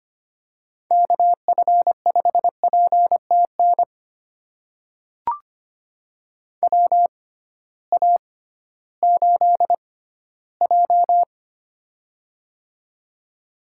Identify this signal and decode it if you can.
Morse code, heard as a keyed tone, about 25 wpm, spaced as for 15 wpm, 700 Hz. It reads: KF5PTN WA8J
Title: USA Call Signs - 6x Character Spacing - 25wpm